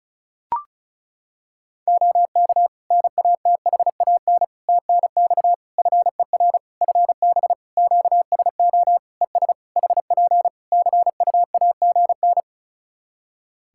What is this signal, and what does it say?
Morse code, heard as a keyed tone, about 36 wpm, 700 Hz. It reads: OK NATHAN TNX FER FB QSO ES HP CUAGN
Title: Practice QSO - Exchange 4 - Ending - 35wpm